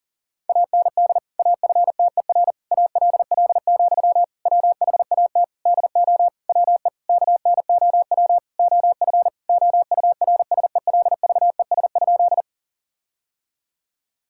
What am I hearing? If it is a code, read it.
Morse code, heard as a keyed tone, about 37 wpm, 700 Hz. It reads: AND AFTER ALL, WHAT DO WE KNOW OF OURSELVES?
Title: Sentences from Top 1600 Words 40wpm